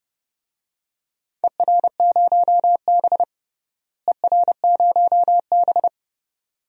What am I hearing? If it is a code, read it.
Morse code, heard as a keyed tone, about 30 wpm, 700 Hz. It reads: ER0B ER0B
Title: DXCC-100 Call Signs - 4 Characters - Repeated 2x - 30wpm